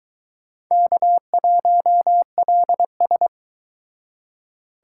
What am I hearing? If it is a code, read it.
Morse code, heard as a keyed tone, about 23 wpm, 700 Hz. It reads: K1LS